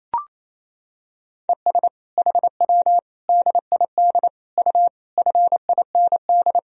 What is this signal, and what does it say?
Morse code, heard as a keyed tone, about 28 wpm, 700 Hz. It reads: ES HW DID U FIND
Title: Sentences from Top 100 Words and Abbreviations 28wpm